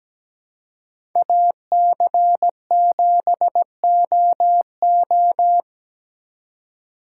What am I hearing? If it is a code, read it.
Morse code, heard as a keyed tone, about 20 wpm, 700 Hz. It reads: AC7OO
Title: Speed Racing - USA 2x2 Call Signs 17wpm